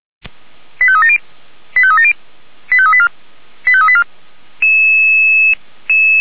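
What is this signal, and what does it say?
Five-tone selective call (ZVEI), heard as quick runs of five tones, tones of about 70 ms. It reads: 85370 85370 85384 85384
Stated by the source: Five-tone ZVEI selective call received by SDR.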